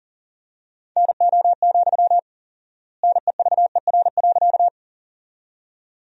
Morse code, heard as a keyed tone, about 40 wpm, 700 Hz. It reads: NO, NEVER.
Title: 4x Word-Spacing - Sentences from Top 300 Words - 40wpm